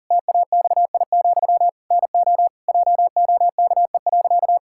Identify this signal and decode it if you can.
Morse code, heard as a keyed tone, about 37 wpm, 700 Hz. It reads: TAXI, NO JOKE.
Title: Pangram Challenge v2 - 40wpm